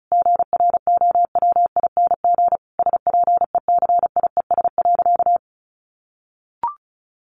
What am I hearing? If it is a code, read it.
Morse code, heard as a keyed tone, about 34 wpm, 700 Hz. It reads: GROWING SPECIES.
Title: A Sound Of Thunder 35wpm